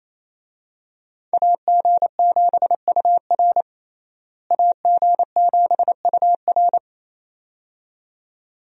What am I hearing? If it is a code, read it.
Morse code, heard as a keyed tone, about 27 wpm, 700 Hz. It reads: AG7UR AG7UR
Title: USA Call Signs - 5 Characters - Repeated 2x - 28wpm